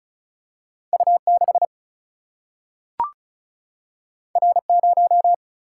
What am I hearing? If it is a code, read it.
Morse code, heard as a keyed tone, about 35 wpm, 700 Hz. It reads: U6 R0
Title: Single-Letter Single-Number 35wpm